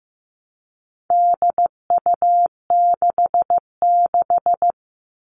Morse code, heard as a keyed tone, about 15 wpm, 700 Hz. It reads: DU66
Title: All Cut Numbers - 4-Digit 15wpm